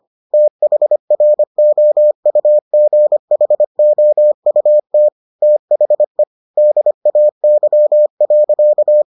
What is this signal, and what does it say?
Morse code, heard as a keyed tone, about 25 wpm, 600 Hz. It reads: THROUGHOUT THE DAY.